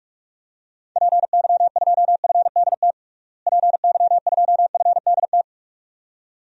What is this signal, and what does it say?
Morse code, heard as a keyed tone, about 47 wpm, 700 Hz. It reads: PY2FDT PY2FDT
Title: DXCC-100 Call Signs - 6 Characters - Repeated 2x - 45wpm